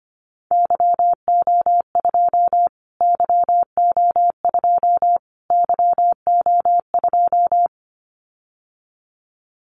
Morse code, heard as a keyed tone, about 25 wpm, 700 Hz. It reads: YO2 YO2 YO2
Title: Callsign Prefixes for Top 25 DXCC - Repeated 3x - 25wpm